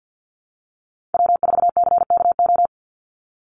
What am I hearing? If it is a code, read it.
Morse code, heard as a keyed tone, about 50 wpm, 700 Hz. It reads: W4CKO
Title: USA Call Signs 50wpm